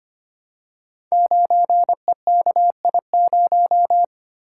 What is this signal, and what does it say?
Morse code, heard as a keyed tone, about 25 wpm, 700 Hz. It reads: 9EKI0
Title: Code Groups - 5 Letter:Numbers - 25wpm